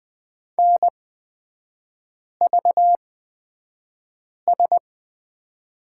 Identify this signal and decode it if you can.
Morse code, heard as a keyed tone, about 20 wpm, spaced as for 5 wpm, 700 Hz. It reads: NVS
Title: 236 Lesson 38e - Letter q - Three Characters 20wpm